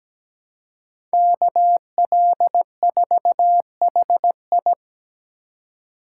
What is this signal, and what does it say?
Morse code, heard as a keyed tone, about 17 wpm, 700 Hz. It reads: KL4HI
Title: USA Call Signs - 4x Character Spacing - 17wpm